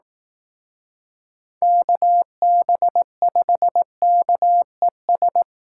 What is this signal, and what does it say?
Morse code, heard as a keyed tone, about 18 wpm, 700 Hz. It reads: KB5KES
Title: Speed Racing - USA 2x3 Call Signs 15wpm